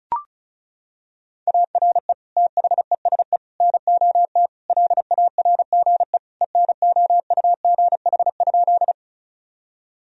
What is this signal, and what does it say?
Morse code, heard as a keyed tone, about 36 wpm, 700 Hz. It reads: ARE THESE NOT LARGE ENOUGH?
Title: Sentences from Top 1000 Words - Review Part 1 - 35wpm